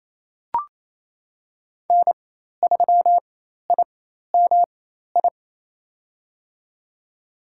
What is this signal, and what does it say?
Morse code, heard as a keyed tone, about 28 wpm, spaced as for 13 wpm, 700 Hz. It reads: N3IMI
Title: USA Call Signs - 4x Character Spacing - 28wpm